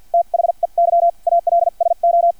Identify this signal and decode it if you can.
Morse code, heard as a keyed tone, about 50 wpm, spaced as for 41 wpm, 680 Hz. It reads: THEY ALSO
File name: archived / CRSnet